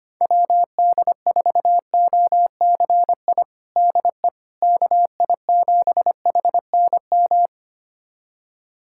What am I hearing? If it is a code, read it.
Morse code, heard as a keyed tone, about 25 wpm, 700 Hz. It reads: WD4OCI DE KI7HNM
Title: USA Callsign DE Callsign - 6 Characters - 25wpm